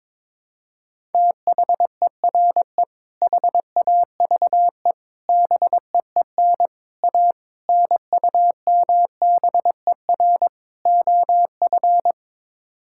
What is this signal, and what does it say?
Morse code, heard as a keyed tone, about 22 wpm, 700 Hz. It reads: THERE HAVE BEEN A NUMBER OF
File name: practice / 22wpm / Sets of 6 Words - 22wpm